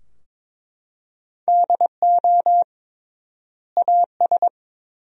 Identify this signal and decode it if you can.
Morse code, heard as a keyed tone, about 21 wpm, 700 Hz. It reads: DO AS
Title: 3x Word-Spacing - Sentences from Top 100 Words - 22wpm